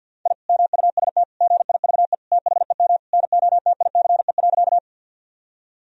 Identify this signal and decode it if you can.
Morse code, heard as a keyed tone, about 50 wpm, 700 Hz. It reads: I MUST GIVE THEM NOTICE.
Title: Sentences from Top 1000 Words - Review Part 2 - 50wpm